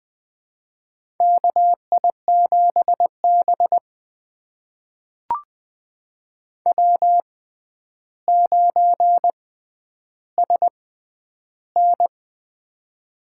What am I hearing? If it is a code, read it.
Morse code, heard as a keyed tone, about 21 wpm, spaced as for 10 wpm, 700 Hz. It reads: KI7B W9SN
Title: USA Call Signs - 4 Characters - 6x Character Spacing - 20wpm